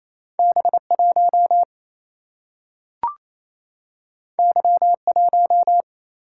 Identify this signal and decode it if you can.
Morse code, heard as a keyed tone, about 28 wpm, 700 Hz. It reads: B 1 Y 1
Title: Single-Letter Single-Number 28wpm